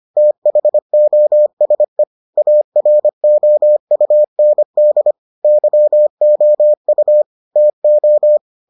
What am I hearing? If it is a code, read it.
Morse code, heard as a keyed tone, about 25 wpm, 600 Hz. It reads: THOSE AROUND YOU TO